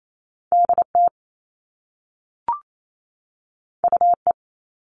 Morse code, heard as a keyed tone, about 28 wpm, 700 Hz. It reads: DT UE
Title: All Cut Numbers - 2-Digit 28wpm